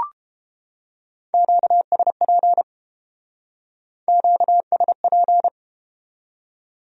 Morse code, heard as a keyed tone, about 32 wpm, 700 Hz. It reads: QSP QSP